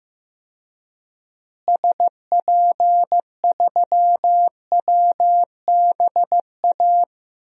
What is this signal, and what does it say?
Morse code, heard as a keyed tone, about 15 wpm, 700 Hz. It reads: SP3WBA
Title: DXCC-100 Call Signs - 6 Characters - Repeated 2x - 15wpm